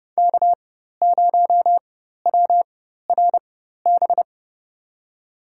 Morse code, heard as a keyed tone, about 30 wpm, spaced as for 14 wpm, 700 Hz. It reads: K0WRB
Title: USA Call Signs - 5 Characters - 4x Character Spacing - 30wpm